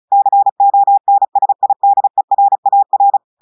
Morse code, heard as a keyed tone, about 36 wpm, 800 Hz. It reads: CONSIDERAR
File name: audios / portugais / portugais texte en clair 2 vitesse 35 mots minutes